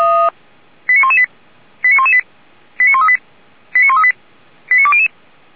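Five-tone selective call (ZVEI), heard as quick runs of five tones, tones of about 80 ms. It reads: 89118 89118 89138 89138 89200
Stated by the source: ZVEI five-tone selective call 89107, received by SDR.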